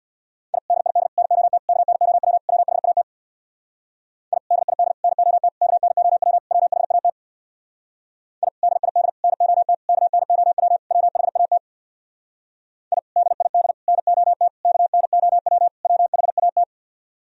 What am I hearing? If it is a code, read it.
Morse code, heard as a keyed tone, about 60 wpm, 700 Hz. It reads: I DID NOT KNOW WHAT I DID NOT KNOW WHAT I DID NOT KNOW WHAT I DID NOT KNOW WHAT
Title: Speed-Racing - Sets of 5 Words 50wpm